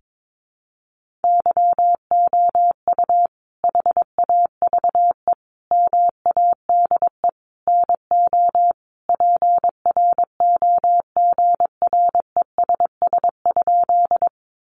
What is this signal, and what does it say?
Morse code, heard as a keyed tone, about 22 wpm, 700 Hz. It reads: YOU HAVE MADE NO PROGRESS?
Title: Sentences from Top 1400 Words 22wpm